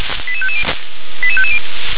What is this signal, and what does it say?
Five-tone selective call (ZVEI), heard as quick runs of five tones, tones of about 70 ms. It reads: 88550 88550